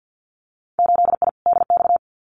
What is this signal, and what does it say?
Morse code, heard as a keyed tone, about 52 wpm, 700 Hz. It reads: 8I DX
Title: Winter Field Day Exchanges Encore 50wpm